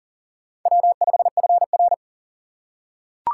Morse code, heard as a keyed tone, about 40 wpm, 700 Hz. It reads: W5FR
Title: USA Call Signs - 4 Characters - 4x Character Spacing - 40wpm